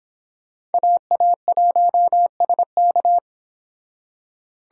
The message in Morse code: AA1SK

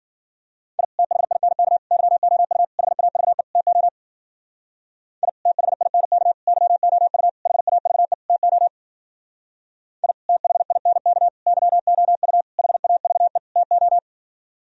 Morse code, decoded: I THINK YOU HAVE TO I THINK YOU HAVE TO I THINK YOU HAVE TO

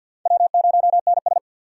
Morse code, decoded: W0NS

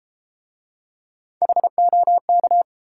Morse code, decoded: HOK

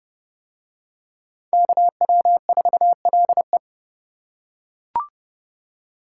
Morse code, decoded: KW4LE

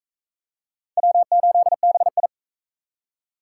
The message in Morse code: W8BI

